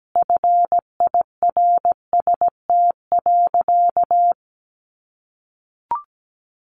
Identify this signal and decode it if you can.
Morse code, heard as a keyed tone, about 17 wpm, 700 Hz. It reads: FIRST.
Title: Sentences from Top 900 Words 17wpm